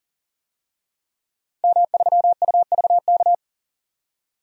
Morse code, heard as a keyed tone, about 40 wpm, 700 Hz. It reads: M3UVK
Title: DXCC-100 Call Signs - 5 Characters - 40wpm